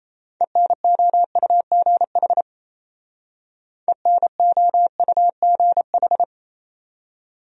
Morse code, heard as a keyed tone, about 37 wpm, 690 Hz. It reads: ENOUGH ENOUGH